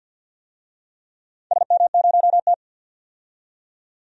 Morse code, heard as a keyed tone, about 52 wpm, 700 Hz. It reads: SM0T
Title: DXCC-100 Call Signs - 4 Characters - 50wpm